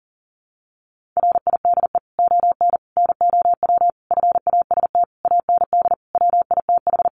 Morse code, heard as a keyed tone, about 40 wpm, 700 Hz. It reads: RIDE ON NOW FAST AND WITH
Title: Sentences from Top 1500 Words 40wpm